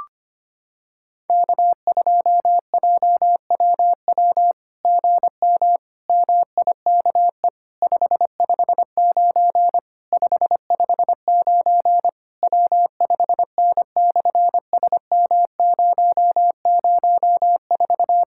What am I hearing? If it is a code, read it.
Morse code, heard as a keyed tone, about 25 wpm, 700 Hz. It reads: K2JWW GM MIKE 559 559 W5N/SM004
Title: SOTA QSO Practice 25wpm